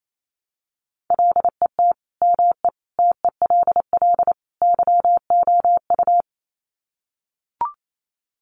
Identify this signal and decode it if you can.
Morse code, heard as a keyed tone, about 28 wpm, 700 Hz. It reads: LET ME TELL YOU
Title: Sets of 4 Words 28wpm